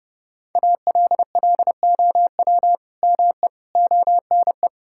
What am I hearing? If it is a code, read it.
Morse code, heard as a keyed tone, about 30 wpm, 700 Hz. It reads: ALLOW ME ONE
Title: Sentences from Top 1000 Words - Review Part 5 - 30wpm